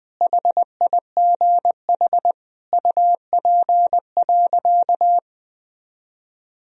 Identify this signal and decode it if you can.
Morse code, heard as a keyed tone, about 20 wpm, 700 Hz. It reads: HIGH UP.
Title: Sentences from Top 1000 Words - Review Part 1 - 20wpm